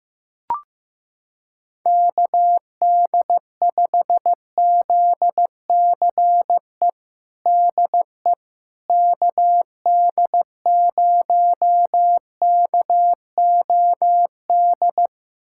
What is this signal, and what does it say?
Morse code, heard as a keyed tone, about 15 wpm, 700 Hz. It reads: KD5ZCE DE KD0KOD
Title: USA Callsign DE Callsign - 6 Characters - 15wpm